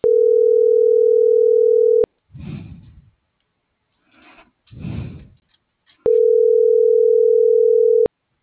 Ringback tone (North America), 440 + 480 Hz.